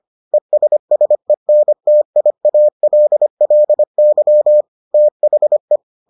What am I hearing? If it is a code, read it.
Morse code, heard as a keyed tone, about 25 wpm, 600 Hz. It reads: ESSENTIALLY THE